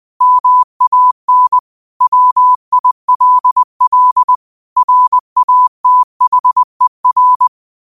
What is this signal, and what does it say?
Morse code, heard as a keyed tone, about 20 wpm, 1000 Hz. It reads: MAN WILL RATHER
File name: 20wpm